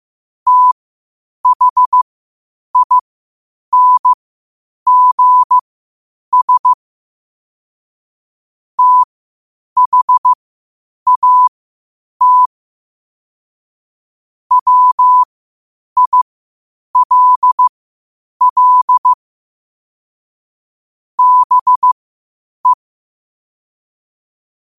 Morse code, heard as a keyed tone, about 15 wpm, spaced as for 8 wpm, 1000 Hz. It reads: THINGS THAT WILL BE
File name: qotdfiles / 8wpm